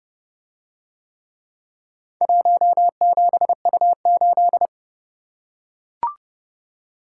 Morse code, heard as a keyed tone, about 27 wpm, 700 Hz. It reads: 17U8